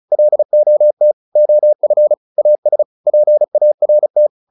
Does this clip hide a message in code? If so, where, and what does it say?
Yes: Morse code from 0.1 to 4.3 s, heard as a keyed tone, about 34 wpm, 600 Hz. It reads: LOT OF AS PART